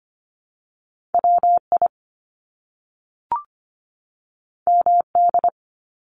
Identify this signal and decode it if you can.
Morse code, heard as a keyed tone, about 25 wpm, 700 Hz. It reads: WI MD
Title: US States 25wpm